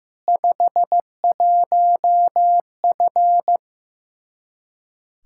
Morse code, heard as a keyed tone, about 16 wpm, 700 Hz. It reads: 51F
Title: Practice QSO - Exchange 2 - Info - 15wpm